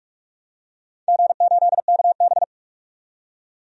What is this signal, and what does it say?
Morse code, heard as a keyed tone, about 46 wpm, 700 Hz. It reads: G8KB
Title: DXCC-100 Call Signs - 4 Characters - 45wpm